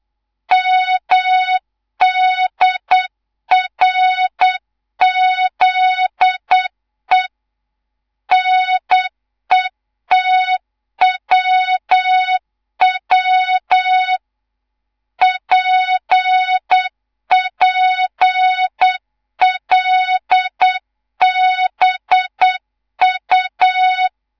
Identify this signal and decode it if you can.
Morse code, heard as a keyed tone, about 8 wpm, 3700 Hz. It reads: MDRZE NETWW PPLBU
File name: Morze-HA2MN / hang / 11 GYAKORLAT